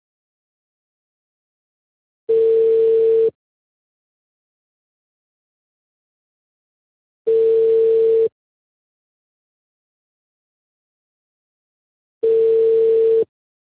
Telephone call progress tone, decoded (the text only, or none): ringback tone